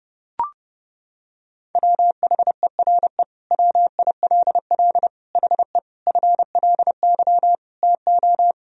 Morse code, heard as a keyed tone, about 30 wpm, 700 Hz. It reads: WHERE WILL HE FLY TO
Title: Sentences from Top 1000 Words - Review Part 2 - 30wpm